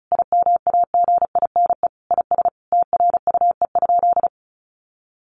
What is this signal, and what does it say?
Morse code, heard as a keyed tone, about 34 wpm, 700 Hz. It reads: IMAGINE IS TRUE?
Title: Sentences from Top 1400 Words 35wpm